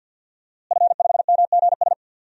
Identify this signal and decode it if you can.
Morse code, heard as a keyed tone, about 51 wpm, 700 Hz. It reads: F5MGS
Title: DXCC-100 Call Signs - 5 Characters - Repeated 3x - 50wpm